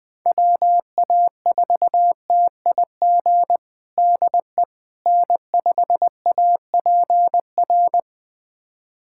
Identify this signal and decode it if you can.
Morse code, heard as a keyed tone, about 20 wpm, 700 Hz. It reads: WA4TIG DE N5APR
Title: USA Callsign DE Callsign 20wpm